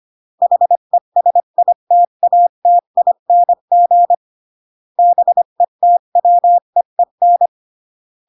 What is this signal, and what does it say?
Morse code, heard as a keyed tone, about 25 wpm, 700 Hz. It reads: HESITATING BETWEEN